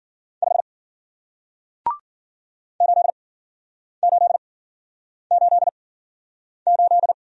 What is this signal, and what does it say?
Morse code, heard as a keyed tone, about 53 wpm, spaced as for 9 wpm, 700 Hz. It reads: 5 8888